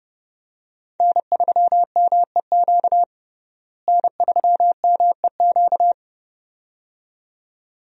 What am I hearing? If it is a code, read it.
Morse code, heard as a keyed tone, about 30 wpm, 700 Hz. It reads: N3MEQ N3MEQ
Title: USA Call Signs - 5 Characters - Repeated 2x - 30wpm